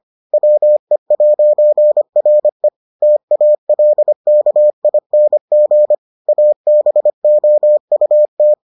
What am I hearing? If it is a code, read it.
Morse code, heard as a keyed tone, about 25 wpm, 600 Hz. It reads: WE'RE TALKING ABOUT